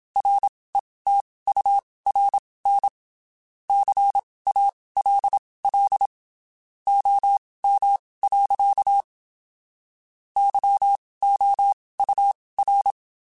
Morse code, heard as a keyed tone, about 27 wpm, spaced as for 19 wpm, 800 Hz. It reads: RETURN CALL OM. YOUR
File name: (spaced) / QSOs Spaced-mp3s / kc7sac @25 sp